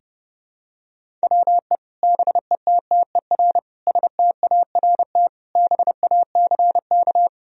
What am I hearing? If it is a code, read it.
Morse code, heard as a keyed tone, about 30 wpm, 700 Hz. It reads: WE BETTER START BACK
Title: Sentences from Top 1000 Words - Review Part 1 - 30wpm